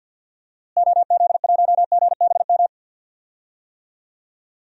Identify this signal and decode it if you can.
Morse code, heard as a keyed tone, about 51 wpm, 700 Hz. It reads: OZ1GBM
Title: DXCC-100 Call Signs - 6 Characters - Repeated 3x - 50wpm